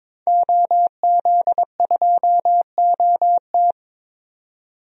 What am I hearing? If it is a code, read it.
Morse code, heard as a keyed tone, about 22 wpm, 700 Hz. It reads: OZ2OT
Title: DXCC-100 Call Signs - 5 Characters - 22wpm